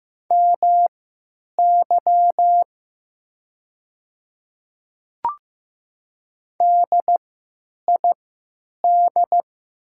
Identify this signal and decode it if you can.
Morse code, heard as a keyed tone, about 15 wpm, spaced as for 8 wpm, 700 Hz. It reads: MY DID
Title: 3x Character Spacing - Top 200 Words - 15wpm